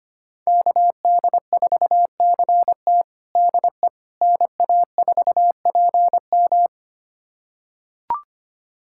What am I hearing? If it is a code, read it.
Morse code, heard as a keyed tone, about 25 wpm, 700 Hz. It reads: KD4CT DE NA4PM
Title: USA Callsign DE Callsign - 5 Characters - 25wpm